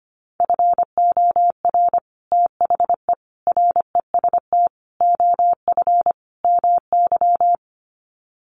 Morse code, heard as a keyed tone, about 25 wpm, 700 Hz. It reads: FOR THE REST OF MY
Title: Sets of 5 Words 25wpm